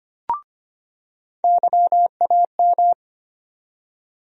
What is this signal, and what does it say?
Morse code, heard as a keyed tone, about 25 wpm, 700 Hz. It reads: YAM
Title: USA Call Sign Postfixes - 3 Letters 25wpm